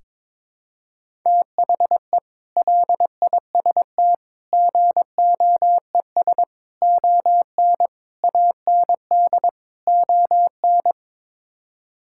Morse code, heard as a keyed tone, about 22 wpm, 700 Hz. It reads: THE LIST GOES ON AND ON
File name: practice / 22wpm / Sets of 6 Words - 22wpm